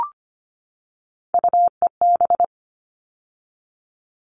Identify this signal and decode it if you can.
Morse code, heard as a keyed tone, about 25 wpm, 700 Hz. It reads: UEB
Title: All Cut Numbers - 3-Digit 25wpm